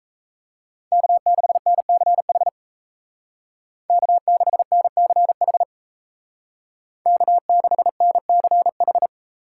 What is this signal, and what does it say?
Morse code, heard as a keyed tone, about 34 wpm, 700 Hz. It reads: K6NCH K6NCH K6NCH
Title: Speed Racing - USA Call Signs 28wpm